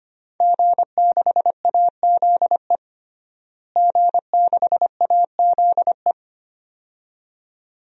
Morse code, heard as a keyed tone, about 25 wpm, 700 Hz. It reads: G6AZE G6AZE
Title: DXCC-100 Call Signs - 5 Characters - Repeated 2x - 25wpm